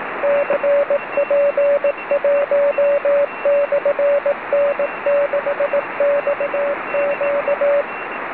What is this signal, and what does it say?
Morse code, heard as a keyed tone, about 18 wpm, 570 Hz, with radio noise. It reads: CP1/N6XQ